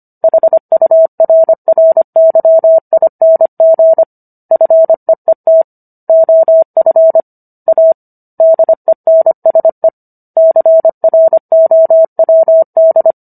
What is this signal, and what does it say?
Morse code, heard as a keyed tone, about 25 wpm, 650 Hz. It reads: HURRYING FEET OF A DENSE CROWD